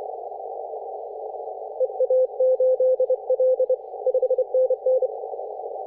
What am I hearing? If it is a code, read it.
Morse code, heard as a keyed tone, about 25 wpm, 510 Hz. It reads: EA8L 5NN
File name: 3D2YA 24.913MHz CW